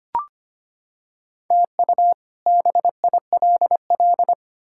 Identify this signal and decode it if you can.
Morse code, heard as a keyed tone, about 25 wpm, 700 Hz. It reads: TU BILL
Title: K1USN SST QSO Practice - 25wpm